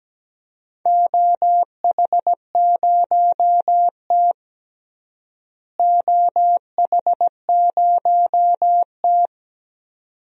Morse code, heard as a keyed tone, about 17 wpm, 700 Hz. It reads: OH0T OH0T
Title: DXCC-100 Call Signs - 4 Characters - Repeated 2x - 17wpm